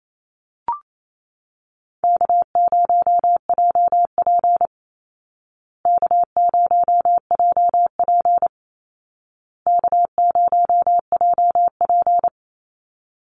Morse code, heard as a keyed tone, about 28 wpm, 700 Hz. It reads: K0JP K0JP K0JP
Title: Easy Call Signs Repeated 3x - 28wpm